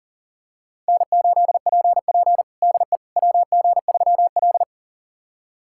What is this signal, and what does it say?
Morse code, heard as a keyed tone, about 37 wpm, 700 Hz. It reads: N8PP DE WG3L
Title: USA Callsign DE Callsign - 4 Characters - 40wpm